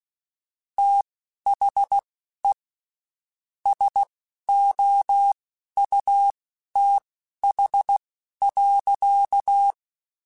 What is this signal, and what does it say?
Morse code, heard as a keyed tone, about 16 wpm, spaced as for 11 wpm, 800 Hz. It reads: THE SOUTH.
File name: Spaced Characters / Spaced Characters-mp3s / Weather4 10 sp @15